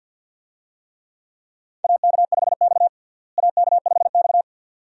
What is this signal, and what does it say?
Morse code, heard as a keyed tone, about 46 wpm, 700 Hz. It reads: AK5X AK5X
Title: USA Call Signs - 4 Characters - Repeated 2x - 50wpm